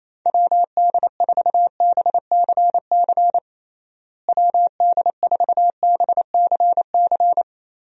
Morse code, heard as a keyed tone, about 28 wpm, 700 Hz. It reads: WD4BCC WD4BCC
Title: USA Call Signs - 6 Characters - Repeated 3x - 28wpm